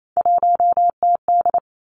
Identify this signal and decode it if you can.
Morse code, heard as a keyed tone, about 28 wpm, 700 Hz. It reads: 1TD
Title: Three Challenging Letter-Number Combinations 28wpm